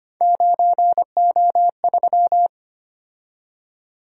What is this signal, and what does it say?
Morse code, heard as a keyed tone, about 25 wpm, 700 Hz. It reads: 9O3
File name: Three Letter-Number Combinations - 25wpm